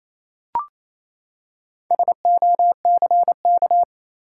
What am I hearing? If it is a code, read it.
Morse code, heard as a keyed tone, about 28 wpm, 700 Hz. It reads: SOCK
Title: Four Letter Words - 28wpm